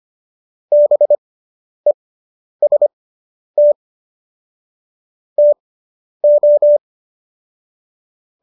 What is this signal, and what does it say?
Morse code, heard as a keyed tone, about 25 wpm, spaced as for 10 wpm, 590 Hz. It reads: BEST TO